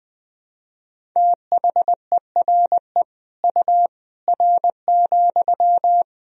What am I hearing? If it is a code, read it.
Morse code, heard as a keyed tone, about 20 wpm, 700 Hz. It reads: THERE U R,
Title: Sentences from Top 100 Words and Abbreviations 20wpm